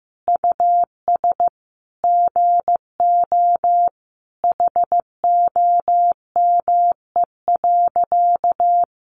Morse code, heard as a keyed tone, about 15 wpm, 700 Hz. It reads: US GO HOME.
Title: Sentences from Top 300 Words 15wpm